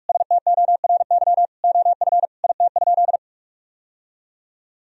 Morse code, heard as a keyed tone, about 46 wpm, 700 Hz. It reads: STORY OF IT?
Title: Sentences from Top 1000 Words - Review Part 1 - 45wpm